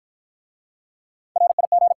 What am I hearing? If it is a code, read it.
Morse code, heard as a keyed tone, about 54 wpm, 700 Hz. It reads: RIG